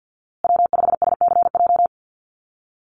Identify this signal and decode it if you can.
Morse code, heard as a keyed tone, about 50 wpm, 700 Hz. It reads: W5SCJ
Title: USA Call Signs Repeated 2x 50wpm